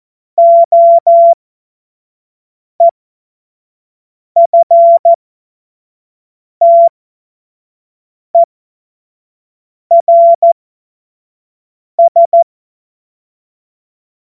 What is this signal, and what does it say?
Morse code, heard as a keyed tone, about 14 wpm, spaced as for 5 wpm, 680 Hz. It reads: OEFTERS
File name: Morsekurs / Witze / Witz 27 Februar 2024 25 F70